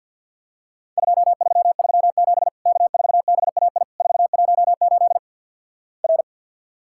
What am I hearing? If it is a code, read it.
Morse code, heard as a keyed tone, about 50 wpm, 700 Hz. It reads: 2337 K4BAI 418 R
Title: Worked All Europe DX Contest - QTC Exchange with Repeats - 50wpm